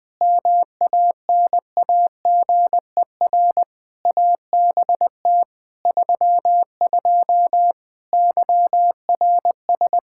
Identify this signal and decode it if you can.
Morse code, heard as a keyed tone, about 20 wpm, 700 Hz. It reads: MANAGER ABT 32 YRS
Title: Practice QSO - Exchange 3 - More Info - 20wpm